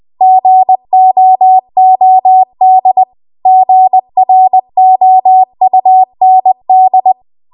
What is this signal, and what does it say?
Morse code, heard as a keyed tone, about 20 wpm, 750 Hz. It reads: GOOD GROUND
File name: Morse / 20WPM